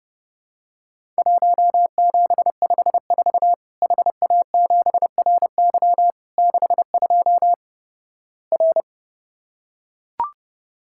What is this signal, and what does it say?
Morse code, heard as a keyed tone, about 30 wpm, 700 Hz. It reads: 1754 HA7RY 62 R
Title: Worked All Europe DX Contest - QTC Exchange - 30wpm